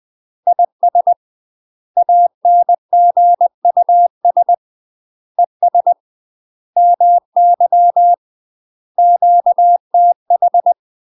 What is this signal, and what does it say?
Morse code, heard as a keyed tone, about 20 wpm, 700 Hz. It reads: IS ANGUS ES MY QTH
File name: UK MASTER QSOs TRAINING / 20WPM / UK MASTER QSOs 0047